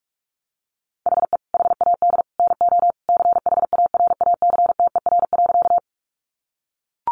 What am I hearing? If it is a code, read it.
Morse code, heard as a keyed tone, about 44 wpm, 700 Hz. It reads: HE HAD NO CHARACTER.
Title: Sentences from Top 1200 Words 45wpm